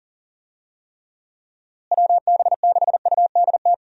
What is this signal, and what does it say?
Morse code, heard as a keyed tone, about 39 wpm, 700 Hz. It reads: WB6UDT
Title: USA Call Signs - 6 Characters - Repeated 3x - 40wpm